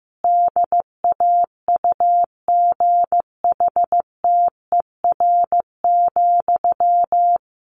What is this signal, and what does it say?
Morse code, heard as a keyed tone, about 15 wpm, 700 Hz. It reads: DAUGHTER,
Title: Sentences from Top 1500 Words 15wpm